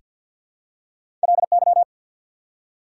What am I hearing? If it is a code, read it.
Morse code, heard as a keyed tone, about 48 wpm, 690 Hz. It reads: LY
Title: Prefixes and Suffixes 50wpm